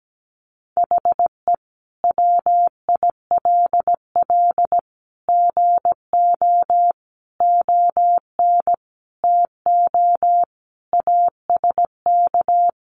Morse code, heard as a keyed tone, about 17 wpm, 700 Hz. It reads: HE WILL GO ON TO ASK
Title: Sentences from Top 1500 Words 17wpm